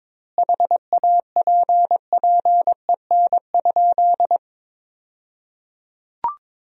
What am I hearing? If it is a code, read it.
Morse code, heard as a keyed tone, about 22 wpm, 700 Hz. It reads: HAPPEN?
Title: 2x Word-Spacing - Sentences from Top 400 Words - 22wpm